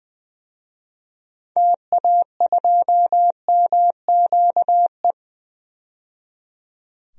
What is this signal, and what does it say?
Morse code, heard as a keyed tone, about 20 wpm, 700 Hz. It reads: TA2MQE
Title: DXCC-100 Call Signs - 6 Characters - 20wpm